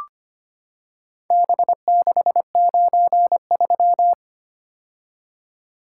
Morse code, heard as a keyed tone, about 25 wpm, 700 Hz. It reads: B693
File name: practice / 25wpm / Four Letter-Number Combinations - 25wpm